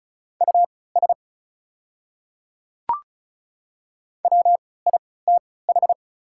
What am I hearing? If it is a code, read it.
Morse code, heard as a keyed tone, about 36 wpm, spaced as for 20 wpm, 690 Hz. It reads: US WITH
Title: 3x Character Spacing - Top 200 Words - 35wpm